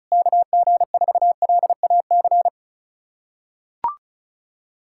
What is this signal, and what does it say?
Morse code, heard as a keyed tone, about 35 wpm, 700 Hz. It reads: KG4LAC
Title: USA Callsign DE Callsign - 6 Characters - 35wpm